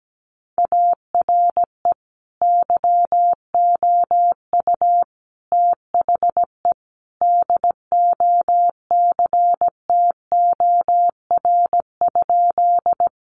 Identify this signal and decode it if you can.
Morse code, heard as a keyed tone, about 17 wpm, 700 Hz. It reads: ARE YOU THE DOCTOR?
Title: Sentences from Top 1500 Words 17wpm